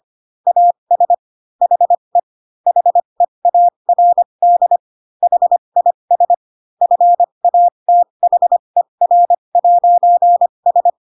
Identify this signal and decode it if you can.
Morse code, heard as a keyed tone, about 25 wpm, spaced as for 22 wpm, 700 Hz. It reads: AS HE HEARD HIS FATHER'S
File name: Pinocchio / Pinocchio ch7 18 25 ik1hge